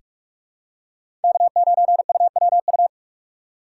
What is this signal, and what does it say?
Morse code, heard as a keyed tone, about 46 wpm, 700 Hz. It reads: K9UWU
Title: USA Call Signs - 5 Characters - 3x Character Spacing - 45wpm